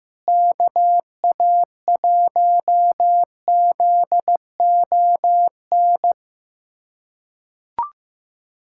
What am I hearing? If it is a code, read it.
Morse code, heard as a keyed tone, about 15 wpm, 700 Hz. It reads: KA1ZON
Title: USA Callsign DE Callsign - 6 Characters - 15wpm